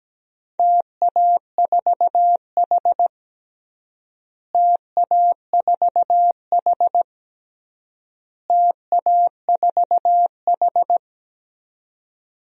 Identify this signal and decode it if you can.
Morse code, heard as a keyed tone, about 17 wpm, 700 Hz. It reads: TA4H TA4H TA4H
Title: DXCC-100 Call Signs - 4 Characters - Repeated 3x - 17wpm